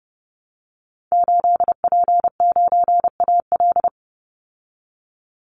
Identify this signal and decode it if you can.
Morse code, heard as a keyed tone, about 30 wpm, 700 Hz. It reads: 8P9AL
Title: Call Signs Repeated 3x - 30wpm